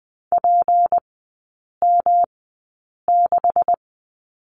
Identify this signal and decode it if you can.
Morse code, heard as a keyed tone, about 20 wpm, spaced as for 8 wpm, 700 Hz. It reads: PM6
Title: 152 Lesson 25e - Letter p - Three Characters 20wpm